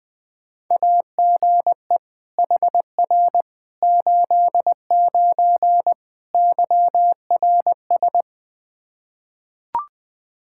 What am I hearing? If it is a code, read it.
Morse code, heard as a keyed tone, about 20 wpm, 700 Hz. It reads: AGE HR 89 YRS
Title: Practice QSO - Exchange 3 - More Info - 20wpm